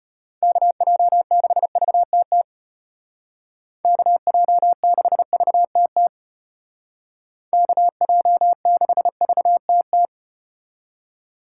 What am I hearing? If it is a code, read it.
Morse code, heard as a keyed tone, about 35 wpm, 700 Hz. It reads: KJ6VTT KJ6VTT KJ6VTT